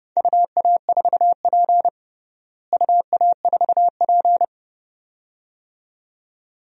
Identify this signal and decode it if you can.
Morse code, heard as a keyed tone, about 30 wpm, 700 Hz. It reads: UA4P UA4P
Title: DXCC-100 Call Signs - 4 Characters - Repeated 2x - 30wpm